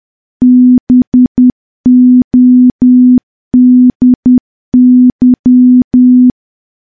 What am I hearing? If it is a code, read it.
Morse code, heard as a keyed tone, about 10 wpm, 250 Hz. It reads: BODY
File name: morse / morse-training-10WPM